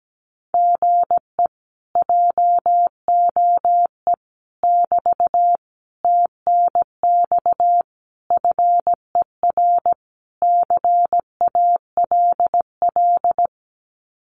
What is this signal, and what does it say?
Morse code, heard as a keyed tone, about 17 wpm, 700 Hz. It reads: GE JOE <BT> TNX FER CALL
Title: Common QSO Phrases Encore 17wpm